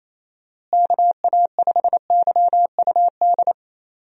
Morse code, heard as a keyed tone, about 28 wpm, 700 Hz. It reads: KA5YUD
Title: USA Call Signs - 6 Characters - Repeated 2x - 28wpm